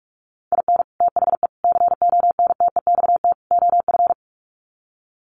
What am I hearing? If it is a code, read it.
Morse code, heard as a keyed tone, about 44 wpm, 700 Hz. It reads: IN THE CONTEXT OF
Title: Sets of 4 Words 45wpm